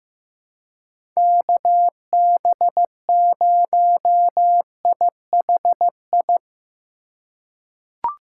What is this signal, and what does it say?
Morse code, heard as a keyed tone, about 16 wpm, 700 Hz. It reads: KB0IHI
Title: USA Call Signs - 2x Character Spacing - 15wpm